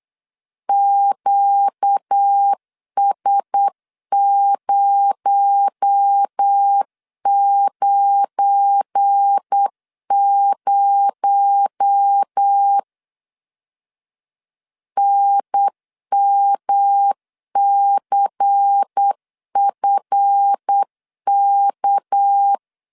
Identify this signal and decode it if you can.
Morse code, heard as a keyed tone, about 8 wpm, 800 Hz. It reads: QS090 NMCFK
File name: LEZIONI / 7 GF9Z / 7 GF9 8wpm 3545